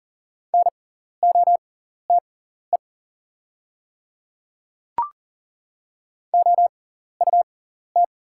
Morse code, heard as a keyed tone, about 40 wpm, spaced as for 14 wpm, 690 Hz. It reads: NOTE OUT
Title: 6x Character Spacing - Top 500 Words - 40wpm